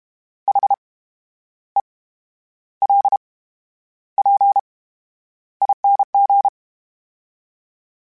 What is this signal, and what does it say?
Morse code, heard as a keyed tone, about 32 wpm, spaced as for 11 wpm, 800 Hz. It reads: HELPING
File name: Special INGs / Special INGs-mp3s / VerySlowToFaster Ing @30